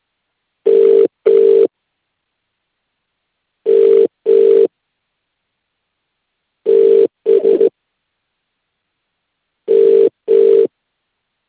Ringback tone (UK), 400 + 450 Hz.